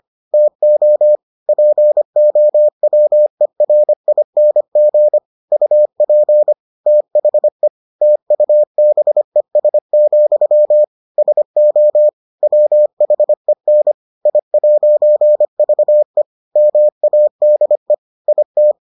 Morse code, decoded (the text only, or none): TO POWERING UP THE TUBES, SO WHEN I'VE MADE IT